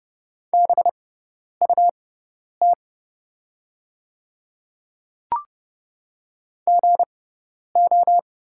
Morse code, heard as a keyed tone, about 30 wpm, spaced as for 10 wpm, 700 Hz. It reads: BUT GO
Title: 6x Character Spacing - Top 100 Words - 30wpm